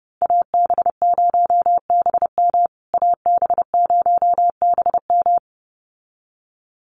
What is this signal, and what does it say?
Morse code, heard as a keyed tone, about 30 wpm, 700 Hz. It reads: AB0BM AB0BM
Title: USA Call Signs Repeated 2x 30wpm